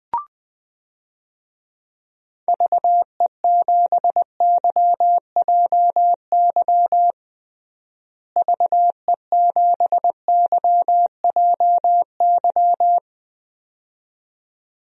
Morse code, heard as a keyed tone, about 20 wpm, 700 Hz. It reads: VE7YJY VE7YJY
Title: DXCC-100 Call Signs - 6 Characters - Repeated 2x - 20wpm